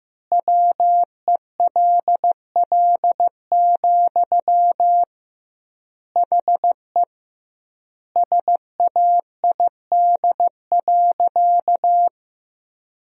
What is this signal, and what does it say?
Morse code, heard as a keyed tone, about 15 wpm, 700 Hz. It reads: WELL, HE SAID.
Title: 2x Word-Spacing - Sentences from Top 200 Words - 15wpm